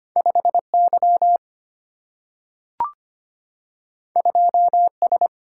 Morse code, heard as a keyed tone, about 25 wpm, 700 Hz. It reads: 5Y 2S
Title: Single-Letter Single-Number 25wpm